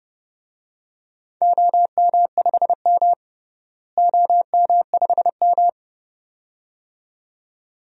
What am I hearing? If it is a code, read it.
Morse code, heard as a keyed tone, about 30 wpm, 700 Hz. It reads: OM5M OM5M
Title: DXCC-100 Call Signs - 4 Characters - Repeated 2x - 30wpm